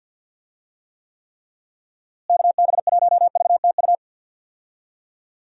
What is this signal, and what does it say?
Morse code, heard as a keyed tone, about 50 wpm, 700 Hz. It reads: KB1VTU
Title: USA Call Signs - 6 Characters - Repeated 3x - 50wpm